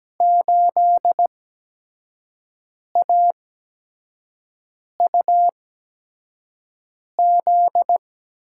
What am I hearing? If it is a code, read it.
Morse code, heard as a keyed tone, about 17 wpm, spaced as for 5 wpm, 700 Hz. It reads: 8AUZ
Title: USA Call Signs - 8x Character Spacing - 17wpm